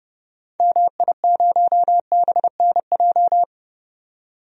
Morse code, heard as a keyed tone, about 32 wpm, 700 Hz. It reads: MI0BNJ
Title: DXCC-100 Call Signs - 6 Characters - 30wpm